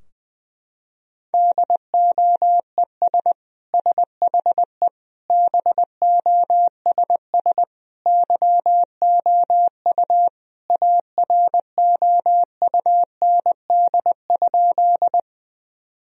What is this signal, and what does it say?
Morse code, heard as a keyed tone, about 20 wpm, 700 Hz. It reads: DOES SHE BOSS YOU AROUND?
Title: Sentences from Top 1000 Words - Review Part 4 - 20wpm